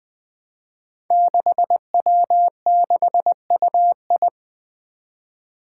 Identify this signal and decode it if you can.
Morse code, heard as a keyed tone, about 20 wpm, 700 Hz. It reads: 6W6UI
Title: Five Challenging Letter-Number Combinations 20wpm